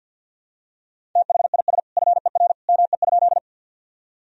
Morse code, heard as a keyed tone, about 50 wpm, 700 Hz. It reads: THIS FER ME?
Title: Sentences from Top 200 Words and Abbreviations 50wpm